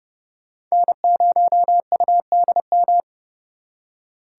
Morse code, heard as a keyed tone, about 36 wpm, 700 Hz. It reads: N0UDM
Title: Speed Racing - USA 1x3 Call Signs 30wpm